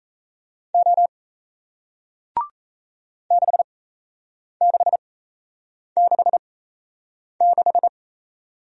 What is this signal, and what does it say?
Morse code, heard as a keyed tone, about 38 wpm, spaced as for 8 wpm, 700 Hz. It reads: O 6666